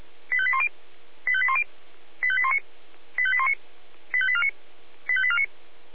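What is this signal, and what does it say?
Five-tone selective call (ZVEI), heard as quick runs of five tones, tones of about 70 ms. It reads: 85810 85810 85819 85819 85849 85849
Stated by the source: ZVEI five-tone selective call received on SDR.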